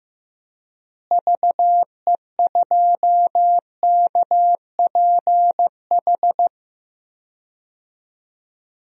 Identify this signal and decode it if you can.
Morse code, heard as a keyed tone, about 15 wpm, 700 Hz. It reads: VE2KPH
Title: DXCC-100 Call Signs - 6 Characters - Repeated 2x - 15wpm